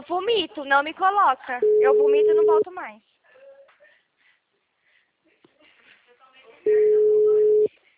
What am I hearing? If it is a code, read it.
Ringback tone (Europe, 425 Hz), with speech.